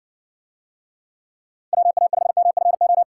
Morse code, heard as a keyed tone, about 55 wpm, 700 Hz. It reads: WA5MFO